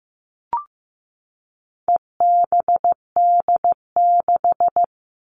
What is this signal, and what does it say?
Morse code, heard as a keyed tone, about 15 wpm, 700 Hz. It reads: EBD6
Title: All Cut Numbers - 4-Digit 15wpm